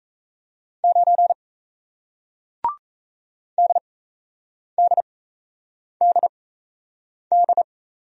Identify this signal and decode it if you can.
Morse code, heard as a keyed tone, about 38 wpm, spaced as for 8 wpm, 700 Hz. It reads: 9 DDDD